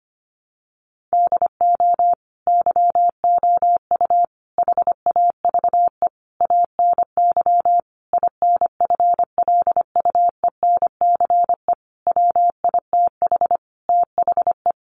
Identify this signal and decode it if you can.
Morse code, heard as a keyed tone, about 25 wpm, 700 Hz. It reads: DO YOU HAVE ANY INFLUENCE WITH THE
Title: Sentences from Top 1100 Words 25wpm